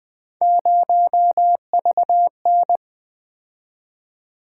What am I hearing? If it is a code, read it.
Morse code, heard as a keyed tone, about 20 wpm, 700 Hz. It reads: 0VN
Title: Three Letter-Number Combinations - 20wpm